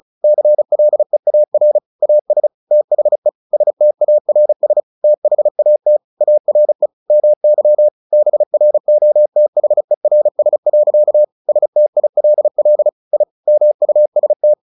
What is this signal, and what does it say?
Morse code, heard as a keyed tone, about 34 wpm, 600 Hz. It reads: CLEAR AS THE STARS THAT ARE MY BROTHERS. STILL I MUST
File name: the old man and the sea 2 35wpm